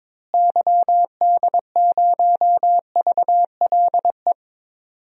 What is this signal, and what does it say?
Morse code, heard as a keyed tone, about 22 wpm, 700 Hz. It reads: YD0VLE
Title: DXCC-100 Call Signs - 6 Characters - Repeated 3x - 22wpm